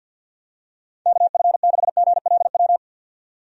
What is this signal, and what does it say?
Morse code, heard as a keyed tone, about 47 wpm, 700 Hz. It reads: KF6GLW
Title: USA Call Signs - 6 Characters - 50wpm